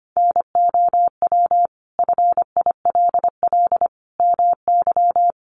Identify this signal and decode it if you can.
Morse code, heard as a keyed tone, about 25 wpm, 700 Hz. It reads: NOW FILL MY
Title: Sentences from Top 1500 Words 25wpm